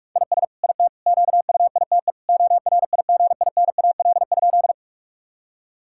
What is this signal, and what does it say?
Morse code, heard as a keyed tone, about 46 wpm, 700 Hz. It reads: IS IT QUITE ORIGINAL?
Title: Sentences from Top 1000 Words - Review Part 6 - 45wpm